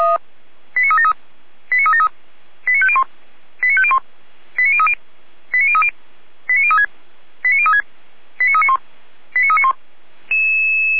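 Five-tone selective call (ZVEI), heard as quick runs of five tones, tones of about 80 ms. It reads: 89373 89373 89501 89501 89030 89030 89036 89036 89391 89391
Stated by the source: ZVEI five-tone selective call received with an SDR.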